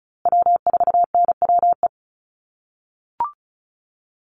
Morse code, heard as a keyed tone, about 34 wpm, 700 Hz. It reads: W4NWE
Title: USA Call Signs Repeated 2x 35wpm